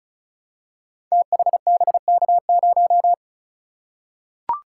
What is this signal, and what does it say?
Morse code, heard as a keyed tone, about 36 wpm, 700 Hz. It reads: THBK0
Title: Code Groups - 5 Letter:Numbers - 35wpm